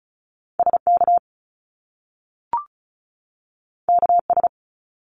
Morse code, heard as a keyed tone, about 36 wpm, 700 Hz. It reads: SK KS
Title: Top Two Letter Combinations in Words 35wpm